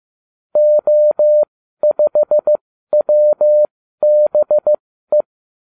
Morse code, heard as a keyed tone, about 15 wpm, spaced as for 12 wpm, 600 Hz. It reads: O5WBE
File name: cw / ubacw32 2